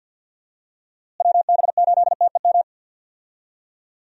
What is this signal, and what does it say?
Morse code, heard as a keyed tone, about 49 wpm, 700 Hz. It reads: WB8TEM
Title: USA Call Signs - 6 Characters - Repeated 3x - 50wpm